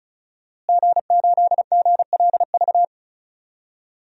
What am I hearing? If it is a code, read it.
Morse code, heard as a keyed tone, about 34 wpm, 700 Hz. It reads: G8GLV
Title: DXCC-100 Call Signs - 5 Characters - Repeated 3x - 35wpm